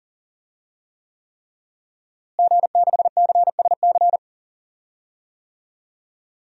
Morse code, heard as a keyed tone, about 40 wpm, 700 Hz. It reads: G6CSC
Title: DXCC-100 Call Signs - 5 Characters - Repeated 3x - 40wpm